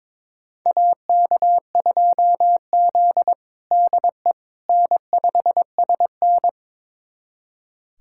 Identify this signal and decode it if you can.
Morse code, heard as a keyed tone, about 22 wpm, 700 Hz. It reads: AK2Z DE N5SN
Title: USA Callsign DE Callsign - 4 Characters - 22wpm